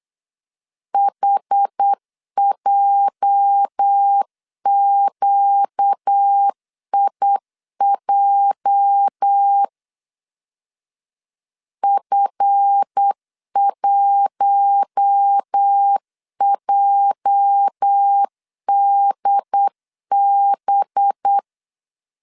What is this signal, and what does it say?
Morse code, heard as a keyed tone, about 8 wpm, 800 Hz. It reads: HJQIJ F1JDB
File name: LEZIONI / 8 YJ45 / 8 YJ 8wpm 3313